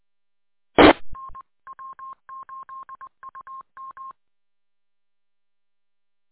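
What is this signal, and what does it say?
Morse code, heard as a keyed tone, about 20 wpm, 1060 Hz. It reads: TN W8UM
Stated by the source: Morse code from the W8UM audio archive on 145.230.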